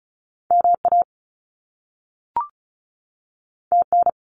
Morse code, heard as a keyed tone, about 36 wpm, 700 Hz. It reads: MA TN
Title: US States 35wpm